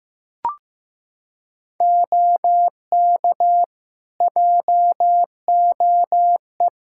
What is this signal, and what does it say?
Morse code, heard as a keyed tone, about 15 wpm, 700 Hz. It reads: OK JOE
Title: Practice QSO - Exchange 3 - More Info - 15wpm